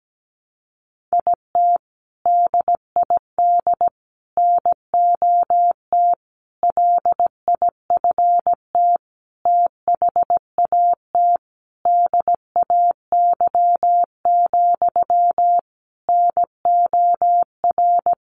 Morse code, heard as a keyed tone, about 17 wpm, 700 Hz. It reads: IT DID NOT LIFT THAT DAY, NOR
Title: Sentences from Top 1500 Words 17wpm